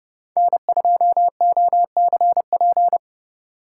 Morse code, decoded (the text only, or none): N2OCP